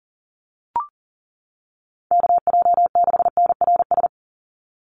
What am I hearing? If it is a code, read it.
Morse code, heard as a keyed tone, about 40 wpm, 700 Hz. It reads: KJ6NRS
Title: USA Call Signs 40wpm